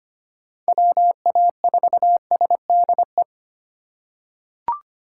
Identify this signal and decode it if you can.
Morse code, heard as a keyed tone, about 25 wpm, 700 Hz. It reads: WA4SDE
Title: USA Call Signs - 6 Characters - 4x Character Spacing - 25wpm